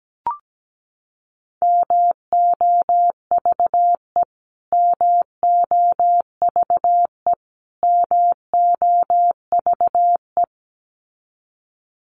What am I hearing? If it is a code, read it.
Morse code, heard as a keyed tone, about 17 wpm, 700 Hz. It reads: MOVE MOVE MOVE
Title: Top 500 Words Repeated 17wpm